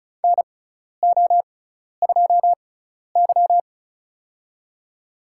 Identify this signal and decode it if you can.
Morse code, heard as a keyed tone, about 36 wpm, spaced as for 12 wpm, 700 Hz. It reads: NO2Y
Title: USA Call Signs - 6x Character Spacing - 35wpm